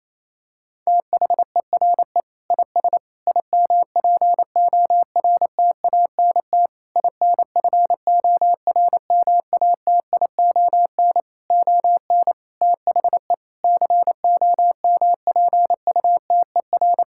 Morse code, heard as a keyed tone, about 28 wpm, 700 Hz. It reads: THERE IS IMPORTANT INFORMATION ON THE COMPUTER
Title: Sentences from Top 1000 Words - Review Part 6 - 28wpm